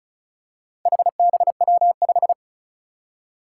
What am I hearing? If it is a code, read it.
Morse code, heard as a keyed tone, about 36 wpm, 700 Hz. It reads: HBW5
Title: Four Challenging Letter-Number Combinations 35wpm